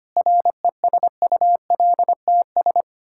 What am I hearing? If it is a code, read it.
Morse code, heard as a keyed tone, about 25 wpm, 700 Hz. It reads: RESULTS
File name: practice / 25wpm / Sentences with 9-Word Phrases - 25wpm